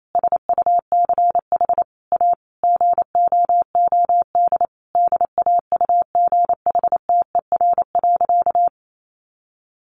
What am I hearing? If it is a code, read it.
Morse code, heard as a keyed tone, about 28 wpm, 700 Hz. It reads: SUCH A GOOD DAUGHTER.
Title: Sentences from Top 1500 Words 28wpm